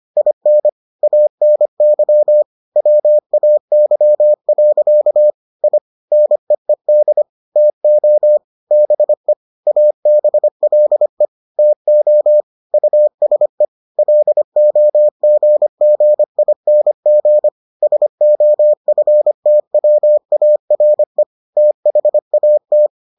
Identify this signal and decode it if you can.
Morse code, heard as a keyed tone, about 25 wpm, 600 Hz. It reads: IN ANY WAY. I NEED TO BE ABLE TO USE LOGGING SOFTWARE THAT